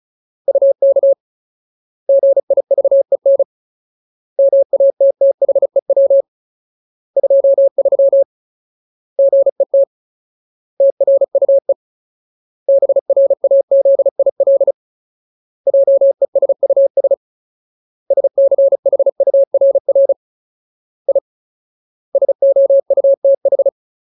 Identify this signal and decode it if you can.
Morse code, heard as a keyed tone, about 34 wpm, 550 Hz. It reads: UK GIVEN MATTHEW 23 GET TRUE BRAZIL JESUS SCHURR I SOUTH
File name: Ultimate Word List / 35WPM Ultimate